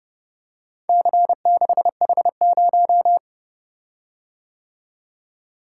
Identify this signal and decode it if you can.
Morse code, heard as a keyed tone, about 29 wpm, 700 Hz. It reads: C6H0